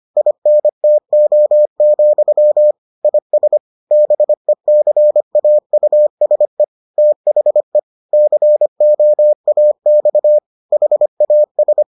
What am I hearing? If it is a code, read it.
Morse code, heard as a keyed tone, about 25 wpm, 600 Hz. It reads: INTO, IS BECAUSE THE COAX HAS